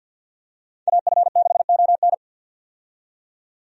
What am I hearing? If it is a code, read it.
Morse code, heard as a keyed tone, about 52 wpm, 700 Hz. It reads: AF6ON
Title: USA Call Signs - 5 Characters - 50wpm